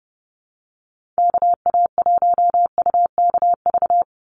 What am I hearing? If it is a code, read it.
Morse code, heard as a keyed tone, about 30 wpm, 700 Hz. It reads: KA1UKV
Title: USA Call Signs 30wpm